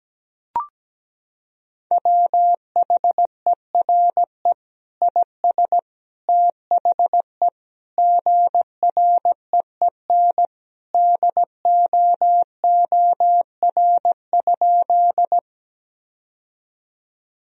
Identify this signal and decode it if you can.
Morse code, heard as a keyed tone, about 17 wpm, 700 Hz. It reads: WHERE IS THE GREEN DOOR?
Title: Sentences from Top 1000 Words - Review Part 2 - 17wpm